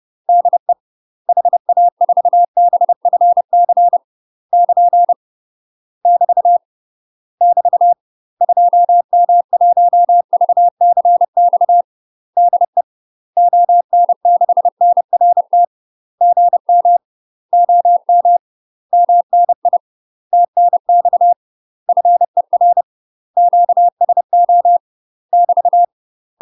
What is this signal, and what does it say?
Morse code, heard as a keyed tone, about 30 wpm, 700 Hz. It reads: DE HA4BFC <KN> <BT> <BT> 2M1VCX DE ON6NRT GM OM MNI TNX FER QSO <BT>
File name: UK MASTER QSOs TRAINING/30WPM